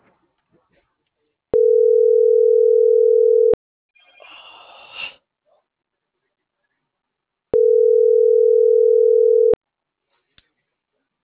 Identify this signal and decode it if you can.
Ringback tone (North America), 440 + 480 Hz.